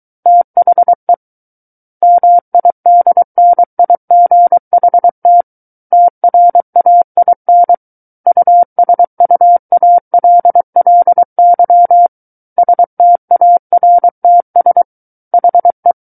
Morse code, heard as a keyed tone, about 23 wpm, 700 Hz. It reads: THE MIDNIGHT TRAIN USUALLY STARTS HE
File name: Book One / 14-Chapter 14 - 23 WPM